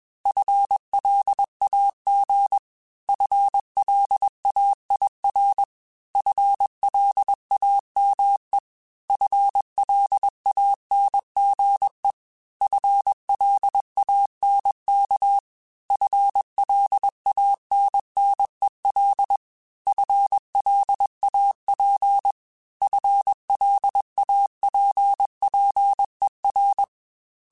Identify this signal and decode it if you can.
Morse code, heard as a keyed tone, about 21 wpm, 800 Hz. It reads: FLAG FLAIR FLAME FLANGE FLANK FLANNEL FLAP FLAPPER
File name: FL Runs / FL Runs mp3s / FL Runs @20